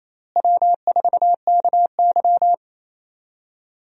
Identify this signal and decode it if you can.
Morse code, heard as a keyed tone, about 28 wpm, 700 Hz. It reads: W4KY
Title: USA Call Signs - 4 Characters - Repeated 2x - 28wpm